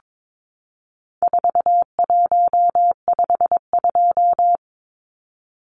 Morse code, heard as a keyed tone, about 22 wpm, 700 Hz. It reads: 4152
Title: Numbers - 4-Digit 22wpm